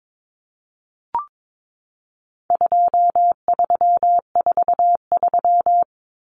Morse code, heard as a keyed tone, about 22 wpm, 700 Hz. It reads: 2343
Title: Numbers - 4-Digit 22wpm